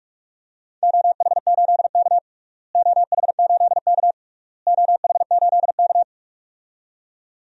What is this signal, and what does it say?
Morse code, heard as a keyed tone, about 47 wpm, 700 Hz. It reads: OH8K OH8K OH8K
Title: DXCC-100 Call Signs - 4 Characters - Repeated 3x - 45wpm